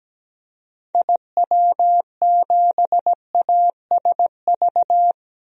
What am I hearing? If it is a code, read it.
Morse code, heard as a keyed tone, about 17 wpm, 700 Hz. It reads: IW7ASV
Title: DXCC-100 Call Signs - 6 Characters - Repeated 2x - 17wpm